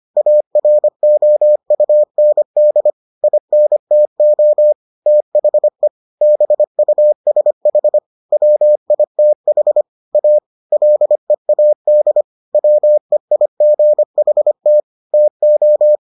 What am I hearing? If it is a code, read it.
Morse code, heard as a keyed tone, about 25 wpm, 600 Hz. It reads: AROUND INTO THE BUSH WITH A LEAD WEIGHT TO